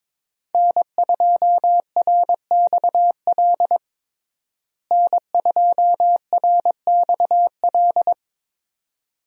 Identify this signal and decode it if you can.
Morse code, heard as a keyed tone, about 22 wpm, 700 Hz. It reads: N2RXL N2RXL
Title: USA Call Signs - 5 Characters - Repeated 2x - 22wpm